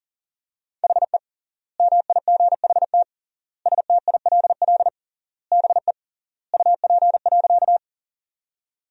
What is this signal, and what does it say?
Morse code, heard as a keyed tone, about 40 wpm, 700 Hz. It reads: HE MIGHT STILL BE UP.
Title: 3x Word-Spacing - Sentences from Top 300 Words - 40wpm